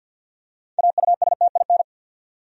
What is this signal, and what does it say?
Morse code, heard as a keyed tone, about 50 wpm, 700 Hz. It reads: AUSTIN
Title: USA Common Names - 50wpm